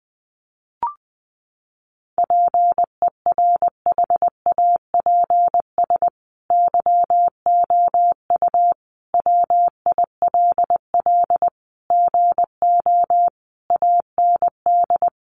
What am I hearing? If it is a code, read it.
Morse code, heard as a keyed tone, about 20 wpm, 700 Hz. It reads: PERHAPS YOU WILL GO AND
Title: Sentences from Top 500 Words 20wpm